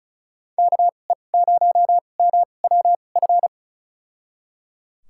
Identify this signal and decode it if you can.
Morse code, heard as a keyed tone, about 34 wpm, spaced as for 25 wpm, 700 Hz. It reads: KE0MWF
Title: USA Call Signs - 6 Characters - 2x Character Spacing - 35wpm